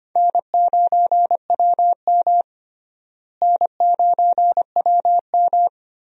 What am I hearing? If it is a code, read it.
Morse code, heard as a keyed tone, about 25 wpm, 700 Hz. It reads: N9WM N9WM
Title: USA Call Signs - 4 Characters - Repeated 2x - 25wpm